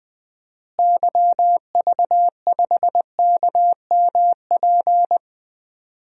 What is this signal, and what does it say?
Morse code, heard as a keyed tone, about 20 wpm, 700 Hz. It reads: YV5KMP
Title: DXCC-100 Call Signs - 6 Characters - Repeated 2x - 20wpm